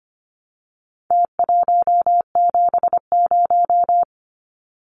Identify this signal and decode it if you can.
Morse code, heard as a keyed tone, about 25 wpm, 700 Hz. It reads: T170
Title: Numbers - Contest Style - 4-Digit 25wpm